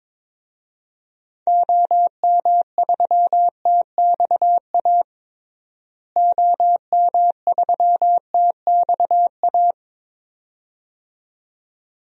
Morse code, heard as a keyed tone, about 21 wpm, 700 Hz. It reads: OM3TXA OM3TXA
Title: DXCC-100 Call Signs - 6 Characters - Repeated 2x - 22wpm